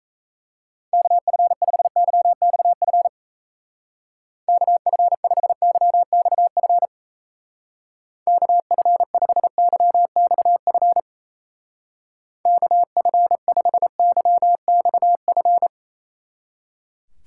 Morse code, heard as a keyed tone, about 35 wpm, 700 Hz. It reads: KF5YXF KF5YXF KF5YXF KF5YXF